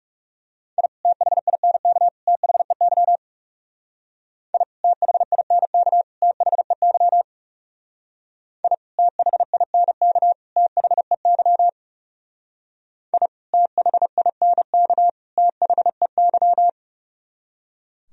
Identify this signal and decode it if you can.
Morse code, heard as a keyed tone, about 38 wpm, 700 Hz. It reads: I THINK THEY I THINK THEY I THINK THEY I THINK THEY